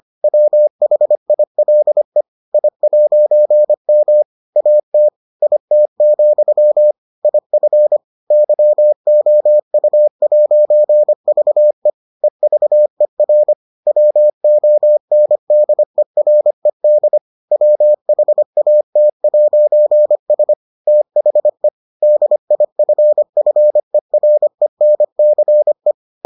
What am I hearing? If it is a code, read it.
Morse code, heard as a keyed tone, about 25 wpm, 600 Hz. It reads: WHILE I'M AT IT, IF YOU'VE EVER WONDERED WHAT'S THE DIFFERENCE